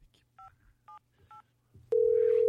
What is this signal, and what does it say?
DTMF, heard as telephone touch tones, tones of about 120 ms. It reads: 578